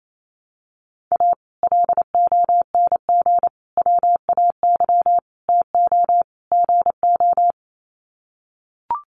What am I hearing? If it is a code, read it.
Morse code, heard as a keyed tone, about 28 wpm, 700 Hz. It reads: A LONG WAY TO GO
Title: Sets of 5 Words 28wpm